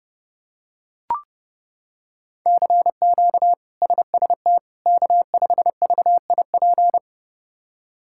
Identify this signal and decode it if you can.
Morse code, heard as a keyed tone, about 30 wpm, 700 Hz. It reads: CQ SST K5VIP
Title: K1USN SST QSO Practice - 30wpm